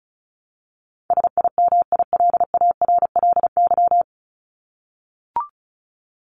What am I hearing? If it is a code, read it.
Morse code, heard as a keyed tone, about 34 wpm, 700 Hz. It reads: SIMILARLY
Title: LY Suffix - Top 100 Words with Base Words 35wpm